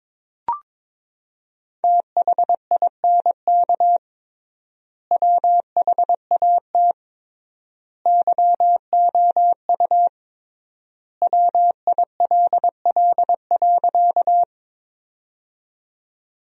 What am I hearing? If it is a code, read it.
Morse code, heard as a keyed tone, about 21 wpm, 700 Hz. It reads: THINK WHAT YOU WILL.
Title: 3x Word-Spacing - Sentences from Top 200 Words - 22wpm